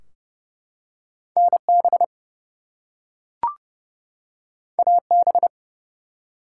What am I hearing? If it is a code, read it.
Morse code, heard as a keyed tone, about 30 wpm, 700 Hz. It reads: NB AB
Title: US States and Canadian Provinces 30wpm